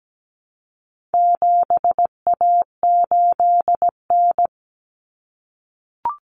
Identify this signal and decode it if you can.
Morse code, heard as a keyed tone, about 17 wpm, 700 Hz. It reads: 7A8N
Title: Common Cut Numbers - 4-Digit 17wpm